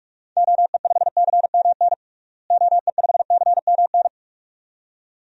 Morse code, heard as a keyed tone, about 41 wpm, 700 Hz. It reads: OE5CMN OE5CMN
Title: DXCC-100 Call Signs - 6 Characters - Repeated 2x - 45wpm